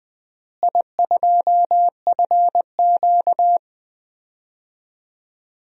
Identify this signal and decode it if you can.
Morse code, heard as a keyed tone, about 20 wpm, 700 Hz. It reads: I2FQ
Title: DXCC-100 Call Signs - 4 Characters - Repeated 3x - 20wpm